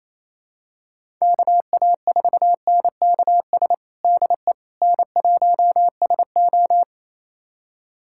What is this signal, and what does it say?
Morse code, heard as a keyed tone, about 28 wpm, 700 Hz. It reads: KA4NKS DE N1SO
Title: USA Callsign DE Callsign 28wpm